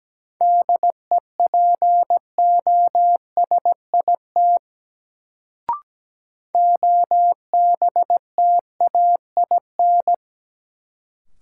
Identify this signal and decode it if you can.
Morse code, heard as a keyed tone, about 17 wpm, 700 Hz. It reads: DEPOSIT OBTAIN
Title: Sentences from Top 1300 Words - New Word Drill - 17wpm